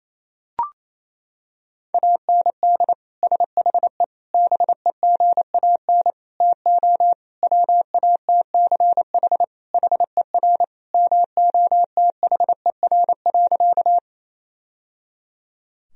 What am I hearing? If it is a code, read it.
Morse code, heard as a keyed tone, about 27 wpm, 700 Hz. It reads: AND SHE BEGAN TO WATCH HER MOTHER.
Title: Sentences from Top 1000 Words - Review Part 1 - 28wpm